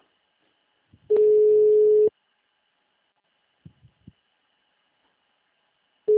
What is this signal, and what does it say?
Ringback tone (Europe, 425 Hz).